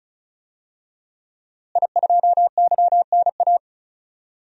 Morse code, heard as a keyed tone, about 34 wpm, 700 Hz. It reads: I2YNA
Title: DXCC-100 Call Signs - 5 Characters - Repeated 2x - 35wpm